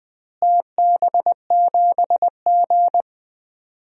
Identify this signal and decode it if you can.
Morse code, heard as a keyed tone, about 21 wpm, 700 Hz. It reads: TB7G
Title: DXCC-100 Call Signs - 4 Characters - Repeated 2x - 20wpm